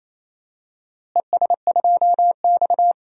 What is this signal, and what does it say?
Morse code, heard as a keyed tone, about 28 wpm, 700 Hz. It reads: ES2X
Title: DXCC-100 Call Signs - 4 Characters - Repeated 2x - 28wpm